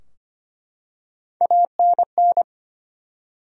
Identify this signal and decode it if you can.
Morse code, heard as a keyed tone, about 25 wpm, 700 Hz. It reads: ANN
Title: USA Common Names - Max 3 Letters - 25wpm